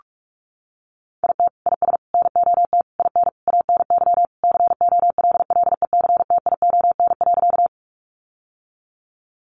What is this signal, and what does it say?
Morse code, heard as a keyed tone, about 44 wpm, 700 Hz. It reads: IT IS NOT IN ANY COLLECTION.
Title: Sentences from Top 1200 Words 45wpm